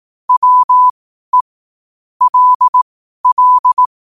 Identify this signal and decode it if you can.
Morse code, heard as a keyed tone, about 18 wpm, spaced as for 12 wpm, 1000 Hz. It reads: WE LL
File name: qotdfiles / 12wpm